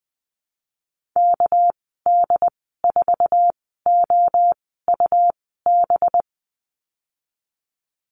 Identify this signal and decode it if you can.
Morse code, heard as a keyed tone, about 19 wpm, spaced as for 14 wpm, 700 Hz. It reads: KD4OUB
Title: 213 Lesson 34g - Number 8 - Call Signs 20wpm